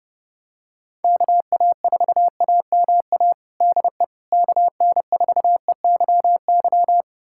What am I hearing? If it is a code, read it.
Morse code, heard as a keyed tone, about 30 wpm, 700 Hz. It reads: KA4AMA DE KN4EYY
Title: USA Callsign DE Callsign 30wpm